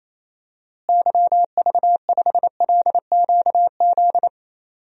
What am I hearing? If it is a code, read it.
Morse code, heard as a keyed tone, about 28 wpm, 700 Hz. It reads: YV5LQZ
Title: DXCC-100 Call Signs - 6 Characters - Repeated 2x - 28wpm